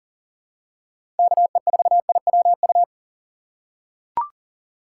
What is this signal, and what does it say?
Morse code, heard as a keyed tone, about 37 wpm, 700 Hz. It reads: KE4IWU
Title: USA Call Signs - 4x Character Spacing - 40wpm